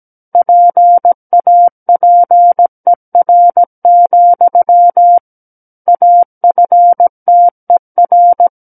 Morse code, heard as a keyed tone, about 17 wpm, 700 Hz. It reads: PAPER, AFTER
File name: Book One / 08-Chapter 8 - 17 WPM